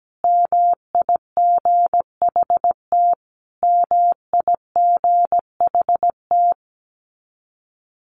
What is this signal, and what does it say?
Morse code, heard as a keyed tone, about 17 wpm, 700 Hz. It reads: MIGHT MIGHT
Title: Top 500 Words Repeated 17wpm